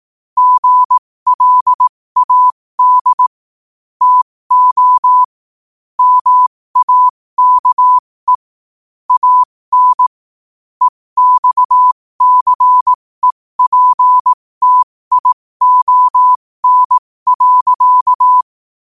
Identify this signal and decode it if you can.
Morse code, heard as a keyed tone, about 18 wpm, spaced as for 15 wpm, 1000 Hz. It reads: GLAD TO MAKE AN EXCEPTION.
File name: qotdfiles / 15wpm